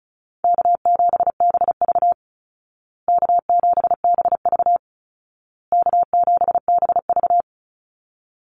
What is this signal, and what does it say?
Morse code, heard as a keyed tone, about 34 wpm, 700 Hz. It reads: K7BV K7BV K7BV
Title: Call Signs Repeated 2x - 35wpm